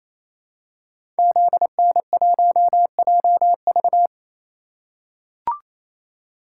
Morse code, heard as a keyed tone, about 29 wpm, 700 Hz. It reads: ZN1JV
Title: Code Groups - 5 Letter:Numbers - 28wpm